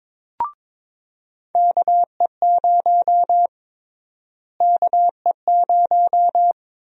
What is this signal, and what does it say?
Morse code, heard as a keyed tone, about 22 wpm, 700 Hz. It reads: KE0 KE0
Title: USA Call Sign Prefixes - Repeated 2x - 22wpm